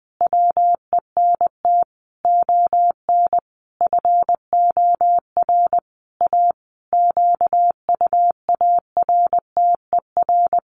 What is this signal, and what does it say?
Morse code, heard as a keyed tone, about 20 wpm, 700 Hz. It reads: WENT ON FOR A QUARTER
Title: Sentences from Top 1500 Words 20wpm